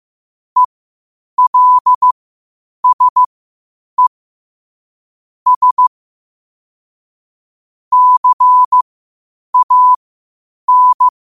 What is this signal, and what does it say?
Morse code, heard as a keyed tone, about 15 wpm, spaced as for 8 wpm, 1000 Hz. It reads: ELSE S CAN
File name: qotdfiles / 8wpm